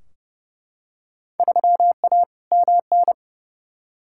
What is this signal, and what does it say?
Morse code, heard as a keyed tone, about 30 wpm, 700 Hz. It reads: 3A MN
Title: Field Day Exchanges - 30wpm